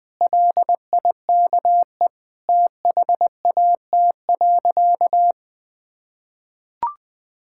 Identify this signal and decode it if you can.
Morse code, heard as a keyed tone, about 20 wpm, 700 Hz. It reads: LIKE THAT.
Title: Sentences with 3-Word Phrases - 20wpm